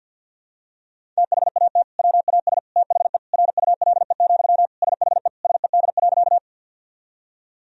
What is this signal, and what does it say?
Morse code, heard as a keyed tone, about 51 wpm, 700 Hz. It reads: THAT WAS THE RULE, SHE SED.
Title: Sentences from Top 400 Words and Abbreviations 50wpm